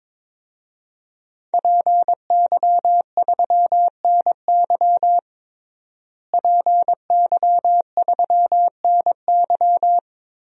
Morse code, heard as a keyed tone, about 22 wpm, 700 Hz. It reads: PY3NY PY3NY
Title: DXCC-100 Call Signs - 5 Characters - Repeated 2x - 22wpm